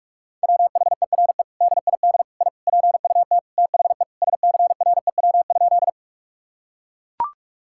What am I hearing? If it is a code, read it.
Morse code, heard as a keyed tone, about 43 wpm, 700 Hz. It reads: WHERE DID I PUT THE SCREW?
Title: Sentences from Top 1600 Words 45wpm